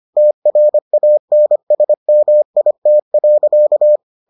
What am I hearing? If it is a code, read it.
Morse code, heard as a keyed tone, about 25 wpm, 600 Hz. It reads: TRANSMIT.